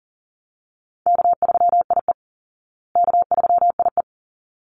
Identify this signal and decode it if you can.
Morse code, heard as a keyed tone, about 40 wpm, 700 Hz. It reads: K3IE K3IE
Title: Call Signs Repeated 3x - 40wpm